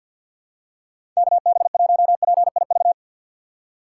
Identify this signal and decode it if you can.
Morse code, heard as a keyed tone, about 52 wpm, 700 Hz. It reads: KB1PIV
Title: USA Call Signs - 3x Character Spacing - 50wpm